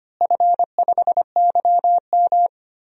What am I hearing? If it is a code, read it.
Morse code, heard as a keyed tone, about 25 wpm, 700 Hz. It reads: F5YM
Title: DXCC-100 Call Signs - 4 Characters - Repeated 2x - 25wpm